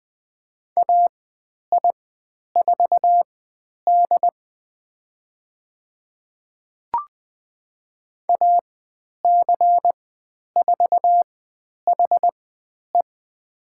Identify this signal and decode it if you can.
Morse code, heard as a keyed tone, about 20 wpm, spaced as for 10 wpm, 700 Hz. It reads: AI4D AC4HE
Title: 102 Lesson 17g - Letter u - Call Signs 20wpm